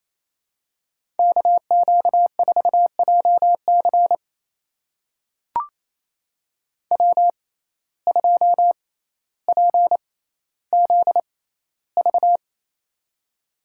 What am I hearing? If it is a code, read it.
Morse code, heard as a keyed tone, about 27 wpm, spaced as for 14 wpm, 700 Hz. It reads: KQ4JC W2PZV
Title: USA Call Signs - 5 Characters - 6x Character Spacing - 28wpm